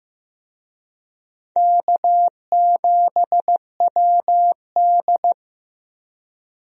Morse code, heard as a keyed tone, about 15 wpm, 700 Hz. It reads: K7WD
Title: USA Call Signs - 4 Characters - Repeated 3x - 15wpm